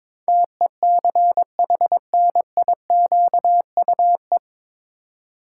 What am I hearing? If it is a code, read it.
Morse code, heard as a keyed tone, about 21 wpm, 700 Hz. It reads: TECHNIQUE
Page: Morse Code Ninja practice 22wpm